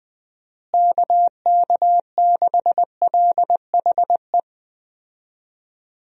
Morse code, heard as a keyed tone, about 20 wpm, 700 Hz. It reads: KK6LHE
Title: USA Call Signs - 6 Characters - Repeated 2x - 20wpm